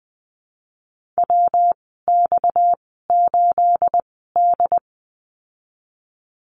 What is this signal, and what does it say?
Morse code, heard as a keyed tone, about 20 wpm, spaced as for 15 wpm, 700 Hz. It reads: WX8D
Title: 238 Lesson 38g - Letter q - Call Signs 20wpm